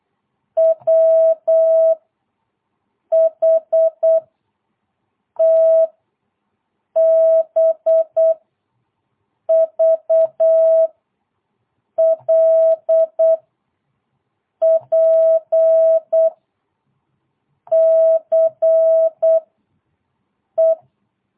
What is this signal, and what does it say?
Morse code, heard as a keyed tone, about 8 wpm, 650 Hz. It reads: W H T B V L P C E
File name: jh8cbh / 2022cw